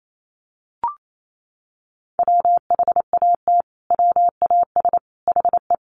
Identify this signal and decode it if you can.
Morse code, heard as a keyed tone, about 28 wpm, 700 Hz. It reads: WHAT WAS HE
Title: Sentences from Top 100 Words 28wpm